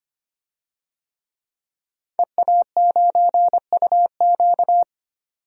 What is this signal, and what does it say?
Morse code, heard as a keyed tone, about 25 wpm, 700 Hz. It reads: EA9UQ
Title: DXCC-100 Call Signs - 5 Characters - 25wpm